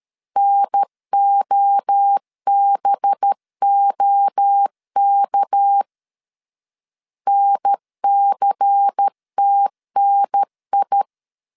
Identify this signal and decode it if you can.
Morse code, heard as a keyed tone, about 13 wpm, 800 Hz. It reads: NOBOK NCTNI
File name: CW / DET / 1 6 1500C S A 12wpm 3440